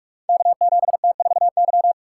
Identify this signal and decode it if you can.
Morse code, heard as a keyed tone, about 45 wpm, 700 Hz. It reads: K7T4Y
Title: Code Groups - 5 Letter:Numbers - 45wpm